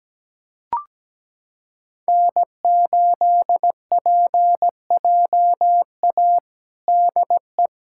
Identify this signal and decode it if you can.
Morse code, heard as a keyed tone, about 17 wpm, 700 Hz. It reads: N8PJA DE
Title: USA Callsign DE Callsign - 5 Characters - 17wpm